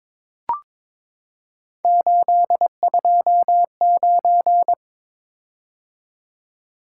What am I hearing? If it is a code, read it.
Morse code, heard as a keyed tone, about 22 wpm, 700 Hz. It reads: 829
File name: practice / 22wpm / Three Letter-Number Combinations - 22wpm